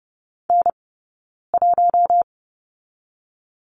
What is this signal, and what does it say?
Morse code, heard as a keyed tone, about 30 wpm, 700 Hz. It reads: N 1
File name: Lesson 20d - Number 3 - Two Characters 30wpm